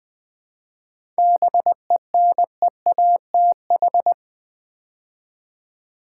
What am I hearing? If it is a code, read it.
Morse code, heard as a keyed tone, about 20 wpm, 700 Hz. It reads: BENEATH
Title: 7 Letter Words - First 3 Letters Given - 20wpm